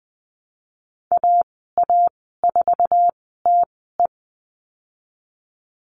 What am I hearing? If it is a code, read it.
Morse code, heard as a keyed tone, about 20 wpm, spaced as for 15 wpm, 700 Hz. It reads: AA4TE
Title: 088 Lesson 15g - Number 5 - Call Signs 20wpm